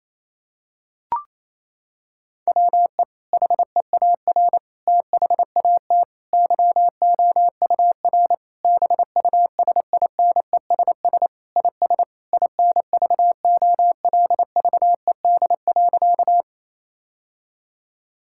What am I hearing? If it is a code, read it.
Morse code, heard as a keyed tone, about 28 wpm, 700 Hz. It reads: WE HEAR THAT YOUR BUSINESS IS INVOLVED.
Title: Sentences from Top 1000 Words - Review Part 5 - 28wpm